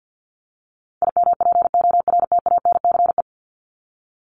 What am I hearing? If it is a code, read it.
Morse code, heard as a keyed tone, about 50 wpm, 700 Hz. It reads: IMPORTANCE
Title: Top 1001-1250 Words 50wpm